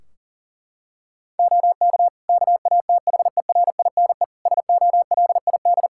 Morse code, heard as a keyed tone, about 40 wpm, 700 Hz. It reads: OK KATHERINE SOLID
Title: Practice QSO - Exchange 3 - More Info - 40wpm